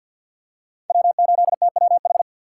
Morse code, heard as a keyed tone, about 52 wpm, 700 Hz. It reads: W8TWH
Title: USA Call Signs - 5 Characters - 6x Character Spacing - 50wpm